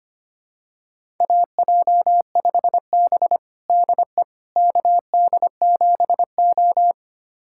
Morse code, heard as a keyed tone, about 25 wpm, 700 Hz. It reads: AJ5B DE KD7O
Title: USA Callsign DE Callsign - 4 Characters - 25wpm